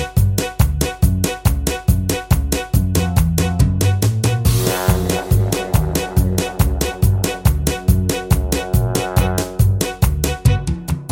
There is no code.